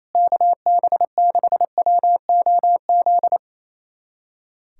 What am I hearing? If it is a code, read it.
Morse code, heard as a keyed tone, about 28 wpm, 700 Hz. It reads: KB6WOZ
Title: USA Callsign DE Callsign 28wpm